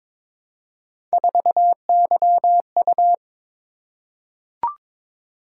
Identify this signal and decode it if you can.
Morse code, heard as a keyed tone, about 22 wpm, 700 Hz. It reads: 4YU
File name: practice / 22wpm / Three Letter-Number Combinations - 22wpm